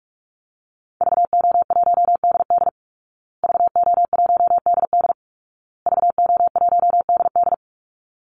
Morse code, heard as a keyed tone, about 44 wpm, 700 Hz. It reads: VO1DD VO1DD VO1DD
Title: Call Signs Repeated 2x - 45wpm